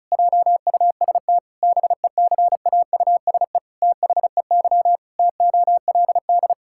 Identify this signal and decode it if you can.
Morse code, heard as a keyed tone, about 34 wpm, 700 Hz. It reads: JUST BECAUSE THEY TOLD
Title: Pangram Challenge v2 - 35wpm